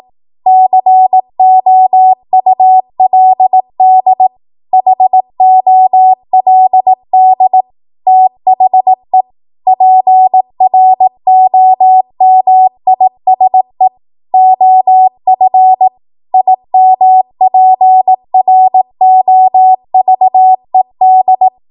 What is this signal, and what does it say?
Morse code, heard as a keyed tone, about 18 wpm, 750 Hz. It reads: COULD HOLD THE PROMISE OF IMPROVED
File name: Morse / 18WPM